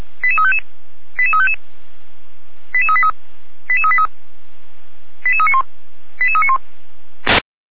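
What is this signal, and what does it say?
Five-tone selective call (ZVEI), heard as quick runs of five tones, tones of about 70 ms. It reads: 88355 88355 88383 88383 88391 88391
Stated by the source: ZVEI five-tone selective call received with an SDR.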